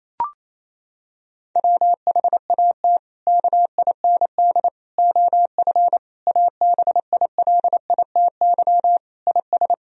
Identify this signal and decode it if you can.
Morse code, heard as a keyed tone, about 28 wpm, 700 Hz. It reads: WHAT KIND OF ABILITY IS
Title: Sentences from Top 1000 Words - Review Part 3 - 28wpm